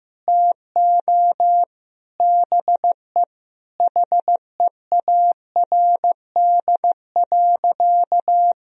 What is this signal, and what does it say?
Morse code, heard as a keyed tone, about 15 wpm, 700 Hz. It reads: TO BE HEARD.
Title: Sentences from Top 1000 Words - Review Part 1 - 15wpm